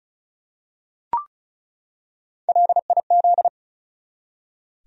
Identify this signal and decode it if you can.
Morse code, heard as a keyed tone, about 34 wpm, 700 Hz. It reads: LIZ
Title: USA Common Names - 35wpm